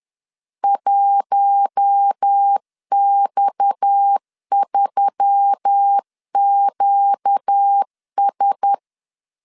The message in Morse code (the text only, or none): 1X3QS